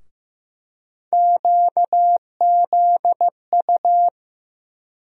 Morse code, heard as a keyed tone, about 15 wpm, 700 Hz. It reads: QZU
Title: USA Call Sign Postfixes - 3 Letters 15wpm